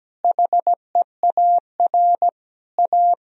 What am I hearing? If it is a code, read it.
Morse code, heard as a keyed tone, about 17 wpm, 700 Hz. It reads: HEAR A
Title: Sentences from Top 1000 Words - Review Part 1 - 17wpm